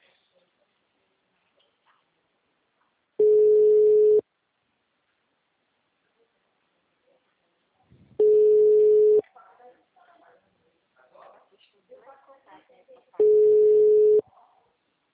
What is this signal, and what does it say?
Ringback tone (Europe, 425 Hz), with speech.